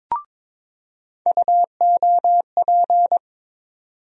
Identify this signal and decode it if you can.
Morse code, heard as a keyed tone, about 21 wpm, 700 Hz. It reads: UOP
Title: USA Call Sign Postfixes - 3 Letters 22wpm